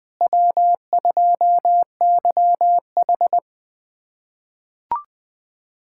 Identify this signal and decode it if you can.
Morse code, heard as a keyed tone, about 20 wpm, 700 Hz. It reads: W2YH
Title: USA Callsign DE Callsign - 4 Characters - 20wpm